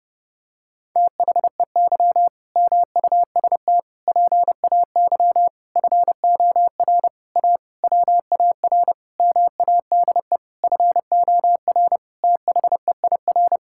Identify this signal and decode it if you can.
Morse code, heard as a keyed tone, about 30 wpm, 700 Hz. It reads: THEY MUST PAY FOR A WAR MADE FOR THEIR
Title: Sentences from Top 1000 Words - Review Part 6 - 30wpm